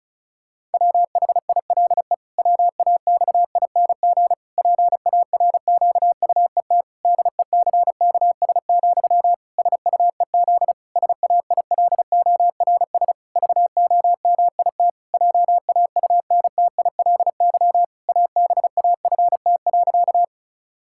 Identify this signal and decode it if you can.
Morse code, heard as a keyed tone, about 36 wpm, 700 Hz. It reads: WHILE WAXING PARQUET DECKS, SUEZ SAILORS VOMIT JAUNTILY ABAFT.
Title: Pangram Challenge v2 - 35wpm